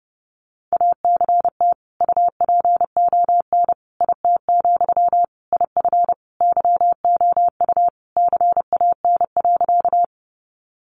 Morse code, heard as a keyed tone, about 30 wpm, 700 Hz. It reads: ACT UPON IT, IF YOU CAN.
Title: Sentences from Top 1100 Words 30wpm